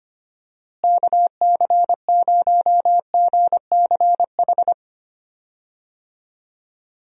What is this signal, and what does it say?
Morse code, heard as a keyed tone, about 25 wpm, 700 Hz. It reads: KC0GCH
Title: USA Call Signs - 6 Characters - Repeated 3x - 25wpm